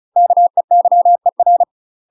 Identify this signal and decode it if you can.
Morse code, heard as a keyed tone, about 36 wpm, 700 Hz. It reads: KEYER
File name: UK MASTER QSOs TRAINING / 35WPM / UK MASTER QSOs 0026